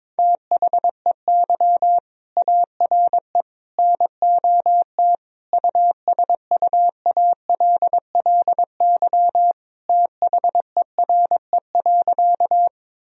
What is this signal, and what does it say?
Morse code, heard as a keyed tone, about 22 wpm, 700 Hz. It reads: THEY ARE NOT USUALLY THERE.
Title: Sentences from Top 1000 Words - Review Part 3 - 22wpm